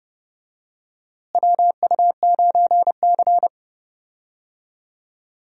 Morse code, heard as a keyed tone, about 30 wpm, 700 Hz. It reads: WU9C
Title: USA Call Signs - 4 Characters - Repeated 2x - 30wpm